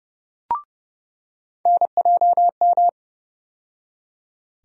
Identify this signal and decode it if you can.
Morse code, heard as a keyed tone, about 30 wpm, 700 Hz. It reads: NJM
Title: USA Call Sign Postfixes - 3 Letters 30wpm